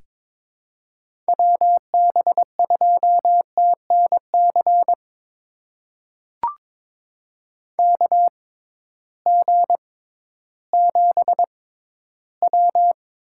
Morse code, heard as a keyed tone, about 22 wpm, spaced as for 13 wpm, 700 Hz. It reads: WB2TNC KG7W
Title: USA Call Signs - 6 Characters - 6x Character Spacing - 22wpm